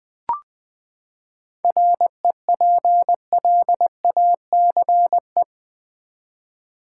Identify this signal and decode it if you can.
Morse code, heard as a keyed tone, about 20 wpm, 700 Hz. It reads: REPLACE
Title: Seven Letter Words - 20wpm